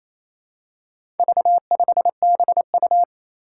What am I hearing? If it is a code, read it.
Morse code, heard as a keyed tone, about 28 wpm, 700 Hz. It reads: V5BU
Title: Four Challenging Letter-Number Combinations 28wpm